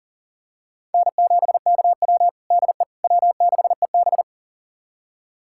Morse code, heard as a keyed tone, about 40 wpm, 700 Hz. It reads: N7KW DE W6EB
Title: USA Callsign DE Callsign - 4 Characters - 40wpm